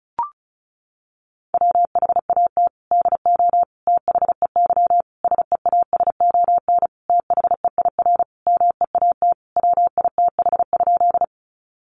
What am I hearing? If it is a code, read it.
Morse code, heard as a keyed tone, about 34 wpm, 700 Hz. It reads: WHAT DO THEY SEASON THEIR MEAT WITH?
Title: Sentences from Top 1400 Words 35wpm